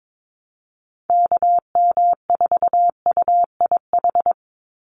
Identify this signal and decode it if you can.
Morse code, heard as a keyed tone, about 22 wpm, 700 Hz. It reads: KM4UIH
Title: USA Call Signs 22wpm